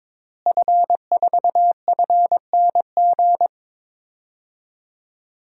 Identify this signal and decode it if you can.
Morse code, heard as a keyed tone, about 21 wpm, 700 Hz. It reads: F4FNG
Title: DXCC-100 Call Signs - 5 Characters - Repeated 2x - 22wpm